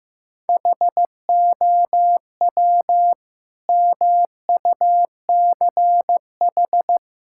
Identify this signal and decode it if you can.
Morse code, heard as a keyed tone, about 15 wpm, 700 Hz. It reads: HOW MUCH
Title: Sentences from Top 1000 Words - Review Part 5 - 15wpm